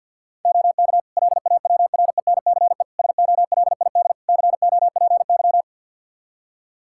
Morse code, heard as a keyed tone, about 47 wpm, 700 Hz. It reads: OK LAWRENCE SOLID COPY
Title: Practice QSO - Exchange 3 - More Info - 50wpm